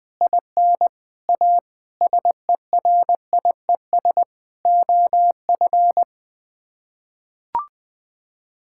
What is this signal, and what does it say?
Morse code, heard as a keyed tone, about 21 wpm, 700 Hz. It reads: IN A SERIES OF
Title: Sets of 8 Words - 20wpm